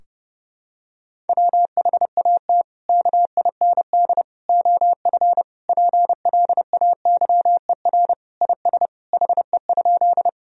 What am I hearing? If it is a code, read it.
Morse code, heard as a keyed tone, about 31 wpm, 700 Hz. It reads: WHAT KIND OF PLAYER IS HE?
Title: Sentences from Top 1000 Words - Review Part 4 - 30wpm